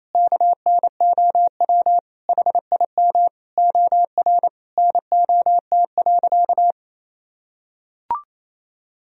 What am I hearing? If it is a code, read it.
Morse code, heard as a keyed tone, about 28 wpm, 700 Hz. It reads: KNOW HIM OR NOT.
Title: Sentences from Top 1000 Words - Review Part 3 - 28wpm